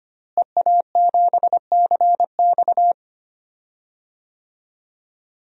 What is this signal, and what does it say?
Morse code, heard as a keyed tone, about 25 wpm, 700 Hz. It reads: EA7CX
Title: DXCC-100 Call Signs - 5 Characters - Repeated 2x - 25wpm